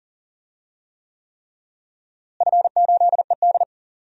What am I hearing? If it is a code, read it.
Morse code, heard as a keyed tone, about 40 wpm, 700 Hz. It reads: F8ED
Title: DXCC-100 Call Signs - 4 Characters - Repeated 3x - 40wpm